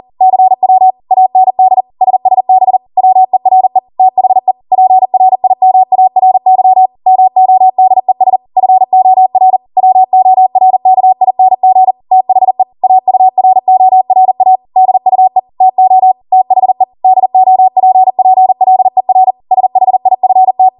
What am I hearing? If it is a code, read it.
Morse code, heard as a keyed tone, about 40 wpm, 750 Hz. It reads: CW AND SSB WERE THE PRIMARY MODES FOR WORKING THE AURORA DUE TO THE DOPPLER SHIFT